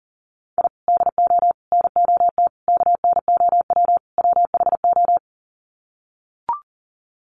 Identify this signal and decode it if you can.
Morse code, heard as a keyed tone, about 40 wpm, 700 Hz. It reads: I DO NOT KNOW WHO
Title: Sets of 5 Words 40wpm